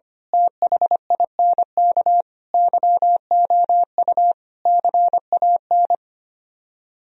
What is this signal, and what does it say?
Morse code, heard as a keyed tone, about 24 wpm, 700 Hz. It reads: THINK YOU CAN
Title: Speed-Racing - Sets of 5 Words 25wpm